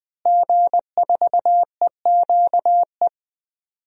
Morse code, heard as a keyed tone, about 20 wpm, 700 Hz. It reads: G4EQE
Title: DXCC-100 Call Signs - 5 Characters - Repeated 2x - 20wpm